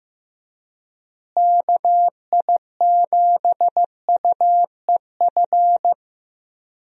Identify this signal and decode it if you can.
Morse code, heard as a keyed tone, about 15 wpm, 700 Hz. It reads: KI7UEF
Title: USA Call Signs - 6 Characters - Repeated 3x - 15wpm